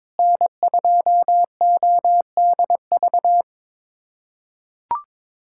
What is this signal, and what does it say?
Morse code, heard as a keyed tone, about 22 wpm, 700 Hz. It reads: N2ODV
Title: USA Callsign DE Callsign - 5 Characters - 22wpm